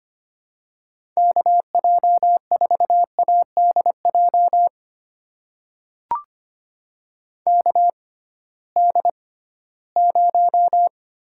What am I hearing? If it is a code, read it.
Morse code, heard as a keyed tone, about 25 wpm, spaced as for 16 wpm, 700 Hz. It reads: KJ4ADJ KD0
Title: USA Call Signs - 6 Characters - 6x Character Spacing - 25wpm